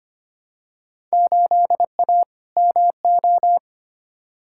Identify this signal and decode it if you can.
Morse code, heard as a keyed tone, about 25 wpm, 700 Hz. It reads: 8A MO
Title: Field Day Exchanges Encore - 25wpm